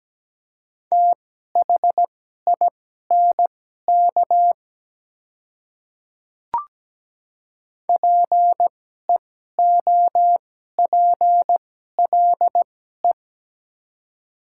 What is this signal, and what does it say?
Morse code, heard as a keyed tone, about 17 wpm, spaced as for 12 wpm, 700 Hz. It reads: THINK PEOPLE
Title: 2x Character Spacing - Top 200 Words - 17wpm